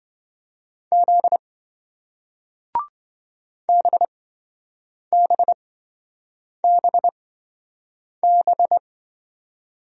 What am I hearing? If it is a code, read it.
Morse code, heard as a keyed tone, about 27 wpm, spaced as for 7 wpm, 700 Hz. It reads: Z BBBB